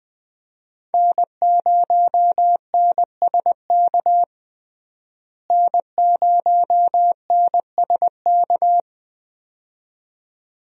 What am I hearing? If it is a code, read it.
Morse code, heard as a keyed tone, about 20 wpm, 700 Hz. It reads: N0NSK N0NSK
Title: USA Call Signs - 5 Characters - Repeated 2x - 20wpm